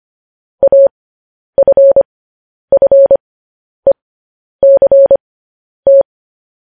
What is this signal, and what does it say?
Morse code, heard as a keyed tone, about 25 wpm, spaced as for 10 wpm, 550 Hz. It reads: AFFECT
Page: news headlines in morse code at 10 wpm